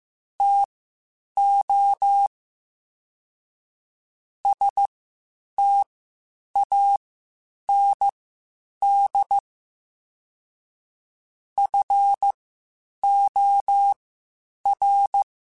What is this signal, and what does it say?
Morse code, heard as a keyed tone, about 15 wpm, spaced as for 8 wpm, 800 Hz. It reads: TO STAND FOR
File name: Quotes Spaced / Quotes Spaced-mp3s / StandFor sp @13